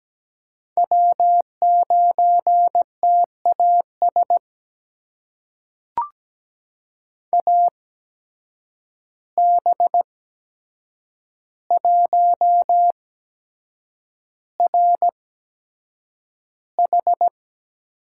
Morse code, heard as a keyed tone, about 17 wpm, spaced as for 7 wpm, 700 Hz. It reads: W9TAS AB1RH
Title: USA Call Signs - 5 Characters - 8x Character Spacing - 17wpm